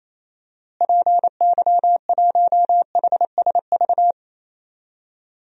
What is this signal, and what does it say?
Morse code, heard as a keyed tone, about 28 wpm, 700 Hz. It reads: PY1HSV
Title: DXCC-100 Call Signs - 6 Characters - Repeated 2x - 28wpm